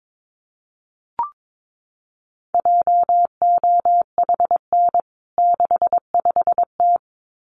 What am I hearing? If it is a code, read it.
Morse code, heard as a keyed tone, about 22 wpm, 700 Hz. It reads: JOHN 65T
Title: CWT Exchange 22wpm